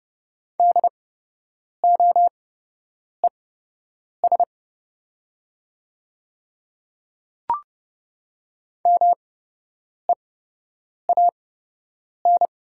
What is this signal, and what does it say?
Morse code, heard as a keyed tone, about 32 wpm, spaced as for 8 wpm, 700 Hz. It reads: DOES MEAN
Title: 8x Character Spacing - Top 200 Words - 30wpm